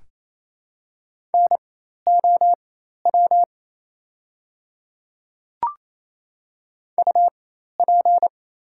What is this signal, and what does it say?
Morse code, heard as a keyed tone, about 27 wpm, spaced as for 13 wpm, 690 Hz. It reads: NOW UP
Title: 4x Character Spacing - Top 100 Words - 28wpm